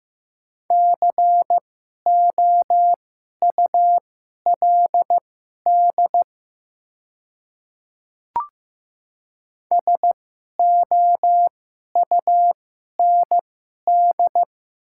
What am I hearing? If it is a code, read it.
Morse code, heard as a keyed tone, about 15 wpm, spaced as for 11 wpm, 700 Hz. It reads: COULD SOUND
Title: 2x Character Spacing - Top 100 Words - 15wpm